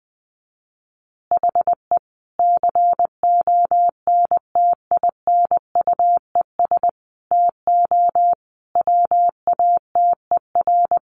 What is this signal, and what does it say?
Morse code, heard as a keyed tone, about 20 wpm, 700 Hz. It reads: HE CONTINUES TO WATER
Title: Easy Phrases 20wpm